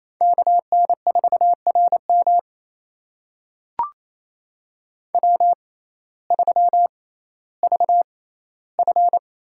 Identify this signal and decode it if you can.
Morse code, heard as a keyed tone, about 27 wpm, spaced as for 15 wpm, 700 Hz. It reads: KN4RM W3VF
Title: USA Call Signs - 5 Characters - 6x Character Spacing - 28wpm